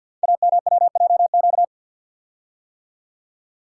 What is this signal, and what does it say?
Morse code, heard as a keyed tone, about 48 wpm, 700 Hz. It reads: AMWJQ
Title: Code Groups - 5 Letters - 50wpm